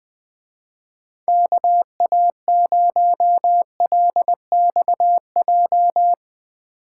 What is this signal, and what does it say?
Morse code, heard as a keyed tone, about 20 wpm, 700 Hz. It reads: KA0LXJ
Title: Speed Racing - USA 2x3 Call Signs 17wpm